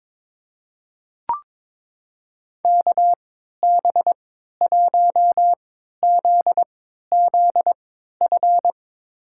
Morse code, heard as a keyed tone, about 22 wpm, spaced as for 13 wpm, 700 Hz. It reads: KB1ZZF
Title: USA Call Signs - 3x Character Spacing - 22wpm